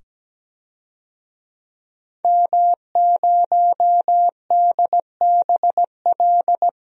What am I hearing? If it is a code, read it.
Morse code, heard as a keyed tone, about 17 wpm, 700 Hz. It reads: M0DBL
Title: DXCC-100 Call Signs - 5 Characters - 17wpm